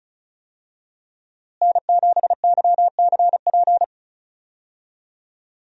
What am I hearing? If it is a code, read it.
Morse code, heard as a keyed tone, about 34 wpm, 700 Hz. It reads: N7YCP
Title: USA Call Signs - 5 Characters - Repeated 2x - 35wpm